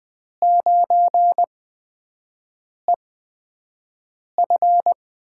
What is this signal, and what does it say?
Morse code, heard as a keyed tone, about 20 wpm, spaced as for 5 wpm, 700 Hz. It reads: 9EF
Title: USA Call Signs - 5 Characters - 8x Character Spacing - 20wpm